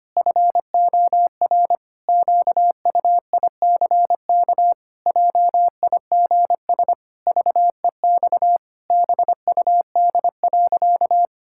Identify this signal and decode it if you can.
Morse code, heard as a keyed tone, about 25 wpm, 700 Hz. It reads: FOR QUICK JIGS VEX BUD.
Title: Pangram Challenge v2 - 25wpm